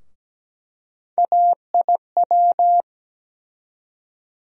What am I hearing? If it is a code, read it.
Morse code, heard as a keyed tone, about 17 wpm, 700 Hz. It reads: AIW
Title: USA Call Sign Postfixes - 3 Letters 17wpm